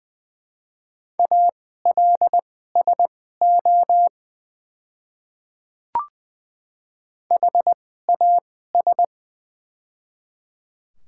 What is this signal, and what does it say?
Morse code, heard as a keyed tone, about 20 wpm, spaced as for 15 wpm, 690 Hz. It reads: ALSO HAS
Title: 2x Character Spacing - Top 200 Words - 20wpm